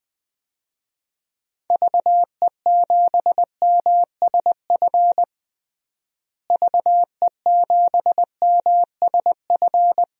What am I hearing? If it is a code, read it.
Morse code, heard as a keyed tone, about 20 wpm, 700 Hz. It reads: VE7MSF VE7MSF
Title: DXCC-100 Call Signs - 6 Characters - Repeated 2x - 20wpm